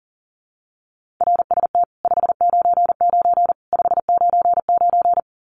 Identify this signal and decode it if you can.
Morse code, heard as a keyed tone, about 40 wpm, 700 Hz. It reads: RST 599 599
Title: Common QSO Phrases 40wpm